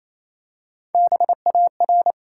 Morse code, heard as a keyed tone, about 29 wpm, 700 Hz. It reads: BAR
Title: Three Letter Words - 28wpm